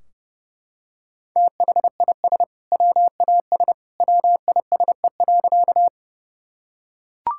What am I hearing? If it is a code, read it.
Morse code, heard as a keyed tone, about 32 wpm, 700 Hz. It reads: THIS WAS WISE.
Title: Sentences from Top 1600 Words 30wpm